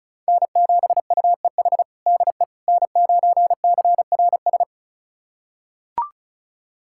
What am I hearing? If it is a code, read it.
Morse code, heard as a keyed tone, about 36 wpm, 700 Hz. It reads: N7UEH DE N9CRS
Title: USA Callsign DE Callsign - 5 Characters - 35wpm